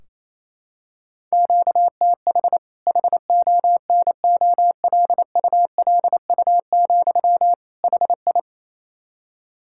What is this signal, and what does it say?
Morse code, heard as a keyed tone, about 28 wpm, 700 Hz. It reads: QTH HONOLULU, HI
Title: K1USN SST QSO Practice 28wpm